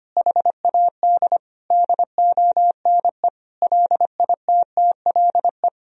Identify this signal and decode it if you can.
Morse code, heard as a keyed tone, about 25 wpm, 700 Hz. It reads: HAD DONE LITTLE
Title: Sentences from Top 1000 Words - Review Part 3 - 25wpm